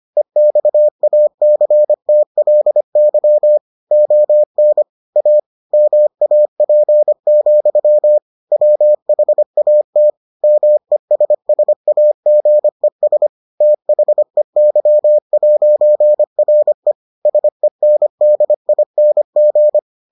Morse code, heard as a keyed tone, about 25 wpm, 600 Hz. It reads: EXACTLY ON A MAP, WHAT MESSAGES THEY'RE SENDING